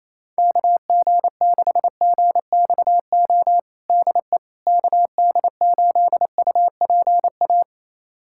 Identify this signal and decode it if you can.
Morse code, heard as a keyed tone, about 28 wpm, 700 Hz. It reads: KG6GXO DE KD8UPA
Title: USA Callsign DE Callsign - 6 Characters - 28wpm